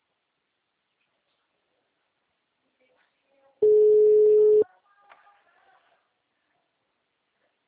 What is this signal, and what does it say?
Ringback tone (Europe, 425 Hz), with speech.
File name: out- -2006- - - .36856